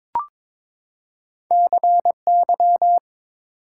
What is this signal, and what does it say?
Morse code, heard as a keyed tone, about 23 wpm, 700 Hz. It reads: CY
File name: practice / 22wpm / Two Letter-Number Combinations - 22wpm